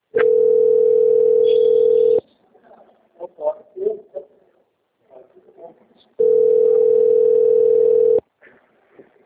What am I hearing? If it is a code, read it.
Ringback tone (North America), 440 + 480 Hz, with speech.
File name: out- - - - - .39438